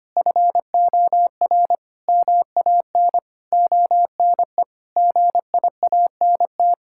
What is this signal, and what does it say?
Morse code, heard as a keyed tone, about 25 wpm, 700 Hz. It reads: FOR MAN ONE GIANT
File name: practice / 25wpm / Sentences with 9-Word Phrases - 25wpm